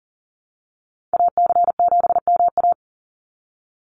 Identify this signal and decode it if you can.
Morse code, heard as a keyed tone, about 40 wpm, 700 Hz. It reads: AC7MA
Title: NA SOTA Activator Callsigns 40wpm